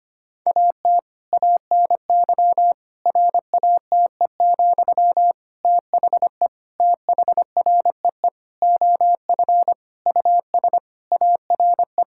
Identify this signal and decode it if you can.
Morse code, heard as a keyed tone, about 25 wpm, 700 Hz. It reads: AT ANY RATE, THE THREE OF US ARE
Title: Sentences from Top 1000 Words - Review Part 6 - 25wpm